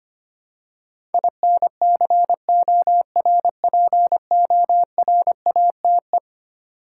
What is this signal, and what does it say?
Morse code, heard as a keyed tone, about 25 wpm, 700 Hz. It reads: INCORPORATE
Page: morse-code-ninja-practice-25wpm